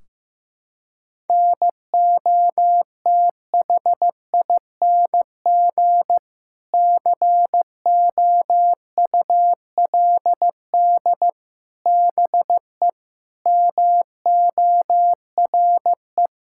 Morse code, heard as a keyed tone, about 15 wpm, 700 Hz. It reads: NOTHING COULD BE MORE
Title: Sentences from Top 1600 Words 15wpm